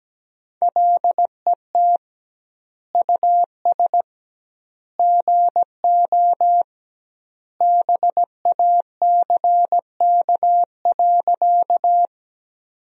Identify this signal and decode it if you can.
Morse code, heard as a keyed tone, about 17 wpm, 700 Hz. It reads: LET US GO BACK.
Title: 2x Word-Spacing - Sentences from Top 300 Words - 17wpm